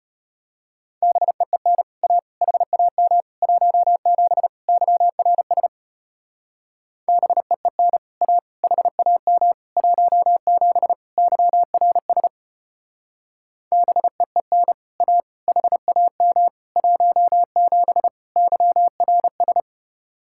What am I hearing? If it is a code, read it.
Morse code, heard as a keyed tone, about 31 wpm, 700 Hz. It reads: BEEN A HAM 17 YRS BEEN A HAM 17 YRS BEEN A HAM 17 YRS
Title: Speed-Racing Common QSO Phrases - 25wpm